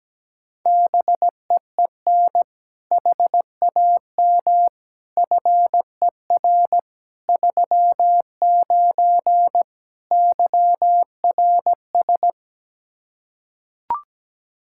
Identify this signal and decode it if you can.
Morse code, heard as a keyed tone, about 17 wpm, 700 Hz. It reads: BEEN HAM FER 39 YRS
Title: Practice QSO - Exchange 3 - More Info - 17wpm